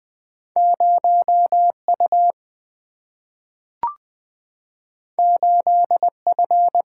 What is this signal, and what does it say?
Morse code, heard as a keyed tone, about 20 wpm, 700 Hz. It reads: 0U 8F
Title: Single-Letter Single-Number 20wpm